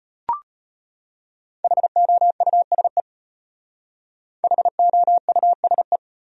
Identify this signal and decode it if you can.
Morse code, heard as a keyed tone, about 31 wpm, 700 Hz. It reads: HOUSE HOUSE
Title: Speed-Racing Top 200 Words 25wpm